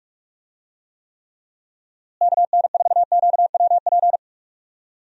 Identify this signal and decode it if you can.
Morse code, heard as a keyed tone, about 45 wpm, 700 Hz. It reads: KN4QWP
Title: USA Call Signs - 6 Characters - Repeated 3x - 45wpm